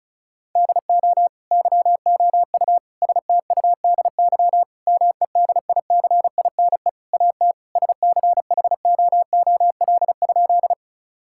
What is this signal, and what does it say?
Morse code, heard as a keyed tone, about 36 wpm, 700 Hz. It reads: DO YOU STUDY MEDICINE AT SCHOOL?
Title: Sentences from Top 1000 Words - Review Part 6 - 35wpm